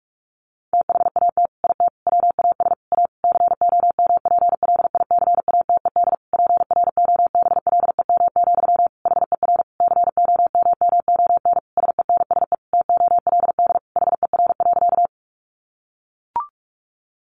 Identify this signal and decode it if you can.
Morse code, heard as a keyed tone, about 44 wpm, 700 Hz. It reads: THAT IT WAS A COMPLICATED PROBLEM, HER COMMON SENSE TOLD HER.
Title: Sentences from Top 1200 Words 45wpm